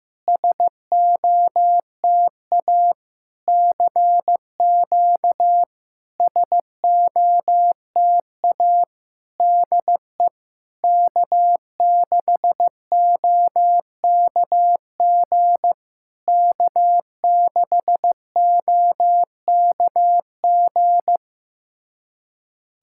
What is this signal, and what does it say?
Morse code, heard as a keyed tone, about 15 wpm, 700 Hz. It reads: SOTA CQ SOTA DE K6OKG K6OKG
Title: SOTA QSO Practice 15wpm